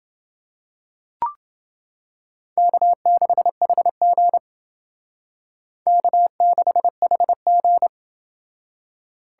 Morse code, heard as a keyed tone, about 28 wpm, 700 Hz. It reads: K6HG K6HG